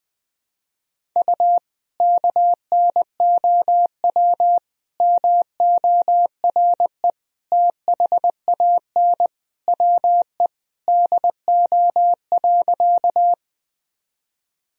Morse code, heard as a keyed tone, about 21 wpm, 700 Hz. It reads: U KNOW MORE THAN WE DO.
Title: Sentences from Top 100 Words and Abbreviations 20wpm